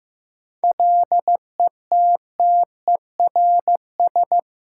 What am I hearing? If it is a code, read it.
Morse code, heard as a keyed tone, about 15 wpm, 700 Hz. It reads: LETTERS
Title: Sentences from Top 1000 Words - Review Part 6 - New Word Drill - 15wpm